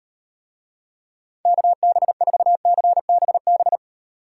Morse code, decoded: KB4CBB

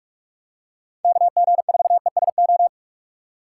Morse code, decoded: KG4ESO

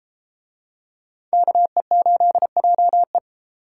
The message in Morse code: KE8JE

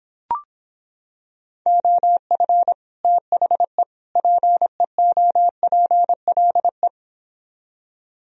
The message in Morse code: OF THE PEOPLE

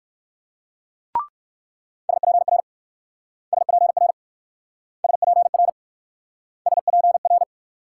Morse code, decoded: SPR SPR SPR SPR

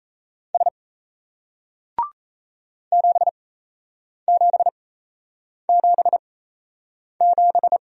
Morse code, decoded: S 7777

